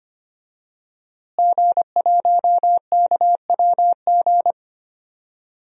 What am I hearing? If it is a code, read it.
Morse code, heard as a keyed tone, about 25 wpm, 700 Hz. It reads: G1KWG
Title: DXCC-100 Call Signs - 5 Characters - Repeated 2x - 25wpm